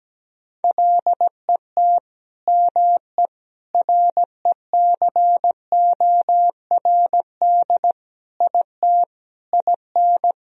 Morse code, heard as a keyed tone, about 17 wpm, 700 Hz. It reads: LET ME RECORD IT IN
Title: Sentences from Top 1000 Words - Review Part 3 - 17wpm